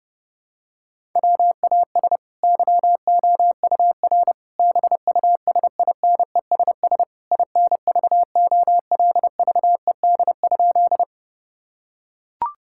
Morse code, heard as a keyed tone, about 30 wpm, 700 Hz. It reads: WAS YOUR BUSINESS INVOLVED?
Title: Sentences from Top 1000 Words - Review Part 5 - 30wpm